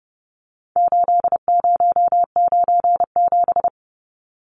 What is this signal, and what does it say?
Morse code, heard as a keyed tone, about 30 wpm, 700 Hz. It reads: 8097
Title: Numbers - 4-Digit 30wpm